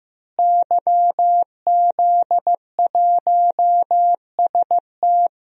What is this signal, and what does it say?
Morse code, heard as a keyed tone, about 15 wpm, 700 Hz. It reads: YZ1ST
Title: DXCC-100 Call Signs - 5 Characters - Repeated 3x - 15wpm